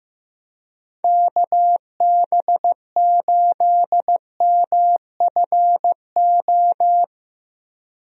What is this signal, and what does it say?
Morse code, heard as a keyed tone, about 15 wpm, 700 Hz. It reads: KB8MFO
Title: USA Call Signs - 6 Characters - Repeated 2x - 15wpm